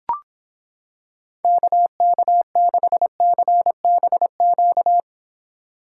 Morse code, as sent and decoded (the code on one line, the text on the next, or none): -.- -.- -.... -.-. -... --.-
KK6CBQ